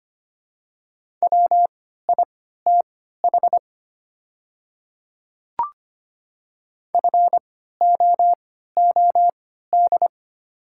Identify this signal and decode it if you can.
Morse code, heard as a keyed tone, about 25 wpm, spaced as for 14 wpm, 700 Hz. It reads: WITH FOOD
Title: 3x Character Spacing - Top 300 Words - 25wpm